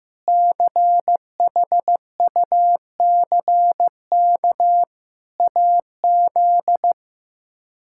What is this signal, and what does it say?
Morse code, heard as a keyed tone, about 15 wpm, 700 Hz. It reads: CHUCK AZ
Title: K1USN SST QSO Practice - 15wpm